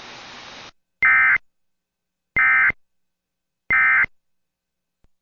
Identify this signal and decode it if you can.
Emergency Alert System (SAME) tones, heard as harsh buzzing bursts. It is heard from 1.0 to 4.0 s.